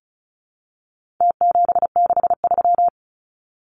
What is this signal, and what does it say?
Morse code, heard as a keyed tone, about 34 wpm, 700 Hz. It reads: T763
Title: Numbers - Contest Style - 4-Digit 35wpm